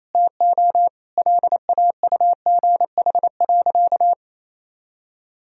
Morse code, heard as a keyed tone, about 28 wpm, 700 Hz. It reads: TO LAUGH.
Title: Sentences from Top 1000 Words - Review Part 1 - 28wpm